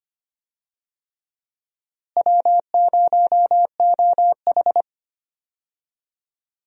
Morse code, heard as a keyed tone, about 25 wpm, 700 Hz. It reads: W0OH
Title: USA Call Signs - 4 Characters - Repeated 3x - 25wpm